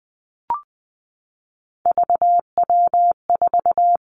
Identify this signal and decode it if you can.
Morse code, heard as a keyed tone, about 20 wpm, 700 Hz. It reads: VW4
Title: Three Challenging Letter-Number Combinations 20wpm